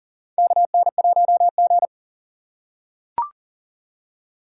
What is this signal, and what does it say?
Morse code, heard as a keyed tone, about 43 wpm, 700 Hz. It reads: KN1G
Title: USA Call Signs - 4x Character Spacing - 40wpm